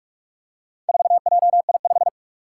Morse code, heard as a keyed tone, about 47 wpm, 700 Hz. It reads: 4JI5
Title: Four Challenging Letter-Number Combinations 45wpm